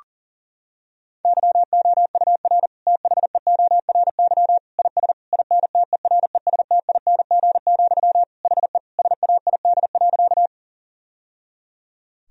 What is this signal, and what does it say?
Morse code, heard as a keyed tone, about 37 wpm, 700 Hz. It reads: YOUR THEORY IS INTERESTING, HE SAID.
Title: Sentences from Top 1000 Words - Review Part 3 - 40wpm